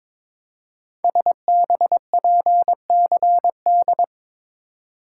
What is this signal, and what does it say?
Morse code, heard as a keyed tone, about 22 wpm, 700 Hz. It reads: SBPCD
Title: Code Groups - 5 Letters - 22wpm